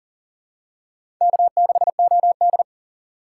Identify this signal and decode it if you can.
Morse code, heard as a keyed tone, about 40 wpm, 700 Hz. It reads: K6OD
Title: USA Call Signs - 6x Character Spacing - 40wpm